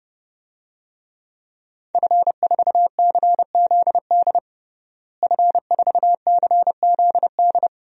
Morse code, heard as a keyed tone, about 30 wpm, 700 Hz. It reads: F4CZD F4CZD
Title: DXCC-100 Call Signs - 5 Characters - Repeated 2x - 30wpm